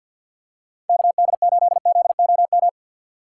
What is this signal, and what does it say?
Morse code, heard as a keyed tone, about 46 wpm, 700 Hz. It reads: KD8ZOM
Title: USA Call Signs - 6 Characters - 6x Character Spacing - 50wpm